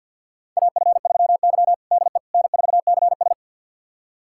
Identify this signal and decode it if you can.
Morse code, heard as a keyed tone, about 51 wpm, 700 Hz. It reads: AF3Y DE N4CS
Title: USA Callsign DE Callsign - 4 Characters - 50wpm